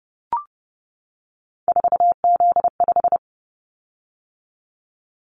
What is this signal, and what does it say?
Morse code, heard as a keyed tone, about 30 wpm, 700 Hz. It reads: 4Z5
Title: Callsign Prefixes for Top 50 DXCC - 30wpm